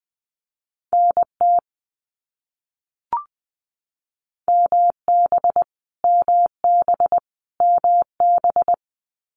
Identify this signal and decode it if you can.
Morse code, heard as a keyed tone, about 20 wpm, 700 Hz. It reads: NT MB MB MB
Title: Canadian Provinces and Territories - Repeated 3x - 20wpm